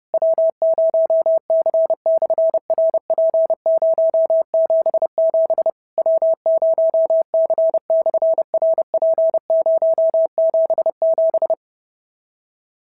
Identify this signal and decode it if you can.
Morse code, heard as a keyed tone, about 32 wpm, 650 Hz. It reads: W0C/RP077 W0C/RP077
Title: SOTA QSO Practice 30wpm